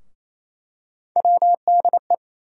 Morse code, heard as a keyed tone, about 27 wpm, 700 Hz. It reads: WDE
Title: USA Call Sign Postfixes - 3 Letters 28wpm